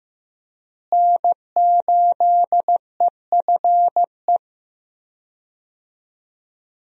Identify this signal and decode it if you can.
Morse code, heard as a keyed tone, about 15 wpm, 700 Hz. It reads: N8EFE
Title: USA Call Signs - 5 Characters - Repeated 3x - 15wpm